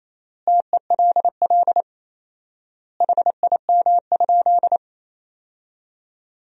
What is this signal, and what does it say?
Morse code, heard as a keyed tone, about 28 wpm, 700 Hz. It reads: TELL HIM?
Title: 4x Word-Spacing - Sentences from Top 200 Words - 28wpm